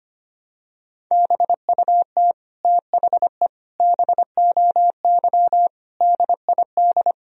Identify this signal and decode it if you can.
Morse code, heard as a keyed tone, about 25 wpm, 700 Hz. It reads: BUT THE BOY DID
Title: Sentences from Top 1000 Words - Review Part 3 - 25wpm